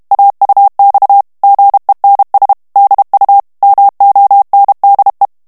Morse code, heard as a keyed tone, about 32 wpm, 800 Hz. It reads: AUX GENS DU MONDE
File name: FRANCAIS TEXTE CLAIR 2 VITESSE 30 MOTS MINUTES